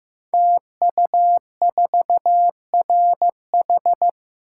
Morse code, heard as a keyed tone, about 15 wpm, 700 Hz. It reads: TU4RH
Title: Code Groups - 5 Letter:Numbers - 15wpm